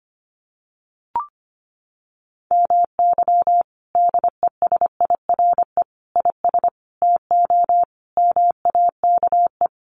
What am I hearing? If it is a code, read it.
Morse code, heard as a keyed tone, about 25 wpm, 700 Hz. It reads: MY DESIRE IS TO MAKE
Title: Sentences from Top 1100 Words 25wpm